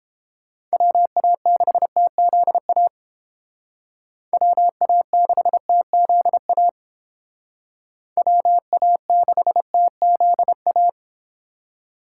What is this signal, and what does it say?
Morse code, heard as a keyed tone, about 30 wpm, 700 Hz. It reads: WA6TZA WA6TZA WA6TZA